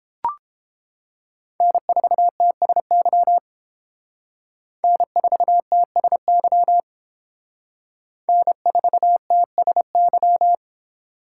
Morse code, heard as a keyed tone, about 28 wpm, 700 Hz. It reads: N4TSY N4TSY N4TSY